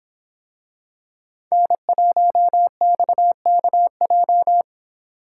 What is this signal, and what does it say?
Morse code, heard as a keyed tone, about 24 wpm, 700 Hz. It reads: N1XKJ